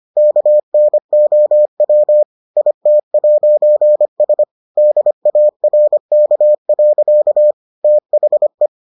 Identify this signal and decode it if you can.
Morse code, heard as a keyed tone, about 25 wpm, 600 Hz. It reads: KNOW IT'S DARK. THE